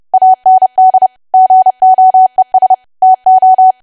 Morse code, heard as a keyed tone, about 30 wpm, 750 Hz. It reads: AND GOES TO
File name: cwa / W1AW-Level3 / 30WPM